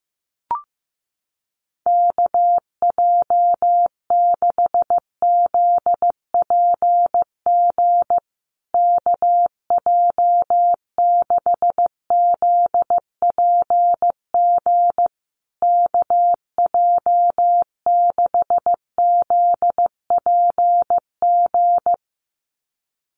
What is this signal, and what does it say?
Morse code, heard as a keyed tone, about 15 wpm, 700 Hz. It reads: KJ6ZPG KJ6ZPG KJ6ZPG
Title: USA Call Signs Repeated 3x 15wpm